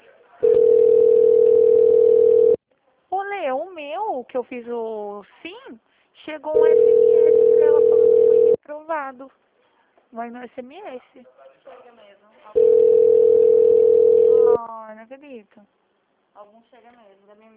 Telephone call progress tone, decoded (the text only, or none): ringback tone